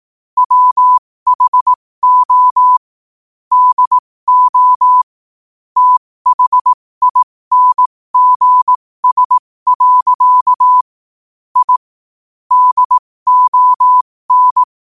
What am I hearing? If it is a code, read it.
Morse code, heard as a keyed tone, about 18 wpm, spaced as for 15 wpm, 1000 Hz. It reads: WHO DO THINGS. I DON
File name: qotdfiles / 15wpm